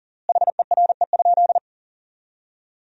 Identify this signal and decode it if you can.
Morse code, heard as a keyed tone, about 40 wpm, 700 Hz. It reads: HERE?
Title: 4x Word-Spacing - Sentences from Top 500 Words - 40wpm